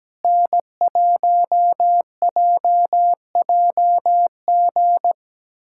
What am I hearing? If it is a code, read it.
Morse code, heard as a keyed tone, about 17 wpm, 700 Hz. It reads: N1JJG
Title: USA Callsign DE Callsign 17wpm